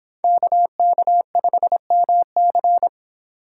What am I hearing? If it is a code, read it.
Morse code, heard as a keyed tone, about 30 wpm, 700 Hz. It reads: KK5MC